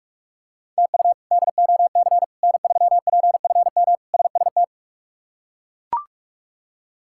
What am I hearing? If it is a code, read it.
Morse code, heard as a keyed tone, about 41 wpm, 700 Hz. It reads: TU DOC N3PFM SST
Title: K1USN SST QSO Practice - 45wpm